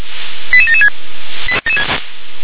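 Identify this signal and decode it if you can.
Five-tone selective call (ZVEI), heard as quick runs of five tones, tones of about 80 ms. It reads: 88776
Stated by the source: ZVEI five-tone call 88795 received on SDR.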